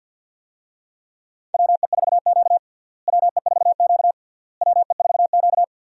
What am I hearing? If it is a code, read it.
Morse code, heard as a keyed tone, about 51 wpm, 700 Hz. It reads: WE4Q WE4Q WE4Q
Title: USA Call Signs - 4 Characters - Repeated 3x - 50wpm